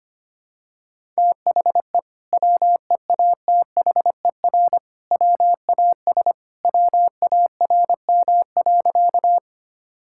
Morse code, heard as a keyed tone, about 25 wpm, 700 Hz. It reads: THE WEATHER WAS WARM.
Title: Sentences from Top 1000 Words - Review Part 5 - 25wpm